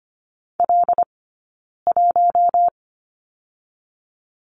Morse code, heard as a keyed tone, about 25 wpm, spaced as for 9 wpm, 700 Hz. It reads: L1
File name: Lesson 15d - Number 5 - Two Characters 25wpm